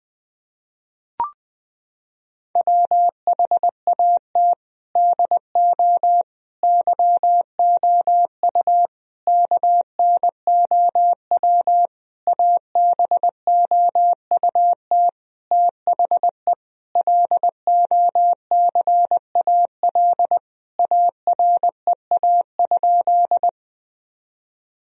Morse code, heard as a keyed tone, about 21 wpm, 700 Hz. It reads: WHAT DO YOU KNOW ABOUT THE LOCAL AREA?
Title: Sentences from Top 1000 Words - Review Part 3 - 20wpm